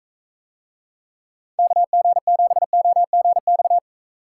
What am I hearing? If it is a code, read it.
Morse code, heard as a keyed tone, about 43 wpm, 700 Hz. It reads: KG7OGX